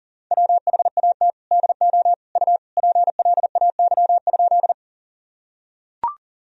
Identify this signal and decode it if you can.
Morse code, heard as a keyed tone, about 40 wpm, 700 Hz. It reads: WHAT DO U PLAY?
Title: Sentences from Top 200 Words and Abbreviations 40wpm